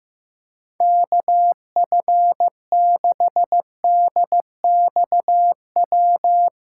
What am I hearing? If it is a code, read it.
Morse code, heard as a keyed tone, about 16 wpm, 700 Hz. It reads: KF6DXW
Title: USA Callsign DE Callsign - 6 Characters - 15wpm